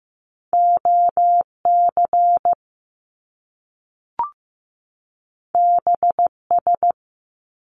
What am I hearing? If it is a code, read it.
Morse code, heard as a keyed tone, about 15 wpm, 700 Hz. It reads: OC BS
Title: Top Two Letter Combinations in Words 15wpm